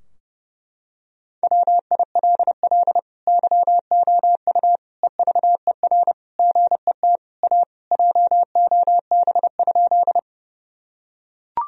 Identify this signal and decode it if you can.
Morse code, heard as a keyed tone, about 30 wpm, 700 Hz. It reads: WILL YOU EVER GET A JOB?
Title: Sentences from Top 1000 Words - Review Part 3 - 30wpm